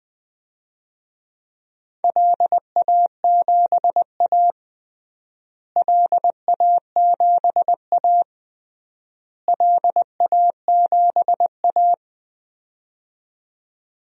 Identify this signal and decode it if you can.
Morse code, heard as a keyed tone, about 20 wpm, 700 Hz. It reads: LA7A LA7A LA7A
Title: DXCC-100 Call Signs - 4 Characters - Repeated 3x - 20wpm